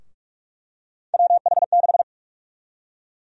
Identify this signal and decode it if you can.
Morse code, heard as a keyed tone, about 46 wpm, 700 Hz. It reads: WH6
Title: USA Call Sign Prefixes - Repeated 2x - 45wpm